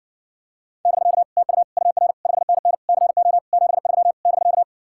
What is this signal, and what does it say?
Morse code, heard as a keyed tone, about 60 wpm, 700 Hz. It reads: <BK> TU UR 5NN CO 73 <BK>